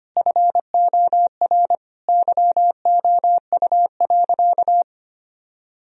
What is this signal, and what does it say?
Morse code, heard as a keyed tone, about 25 wpm, 700 Hz. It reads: FOR YOU.
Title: Sentences from Top 1000 Words - Review Part 4 - 25wpm